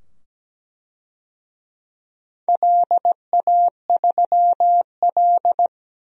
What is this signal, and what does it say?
Morse code, heard as a keyed tone, about 17 wpm, 700 Hz. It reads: LA3L
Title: DXCC-100 Call Signs - 4 Characters - 17wpm